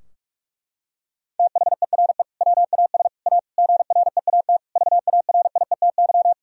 Morse code, heard as a keyed tone, about 41 wpm, 700 Hz. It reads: THERE WAS A GREAT VARIETY
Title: Sentences from Top 1000 Words - Review Part 4 - 45wpm